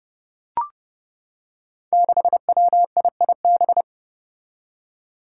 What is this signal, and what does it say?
Morse code, heard as a keyed tone, about 30 wpm, 700 Hz. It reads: 6WIIB
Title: Five Challenging Letter-Number Combinations 30wpm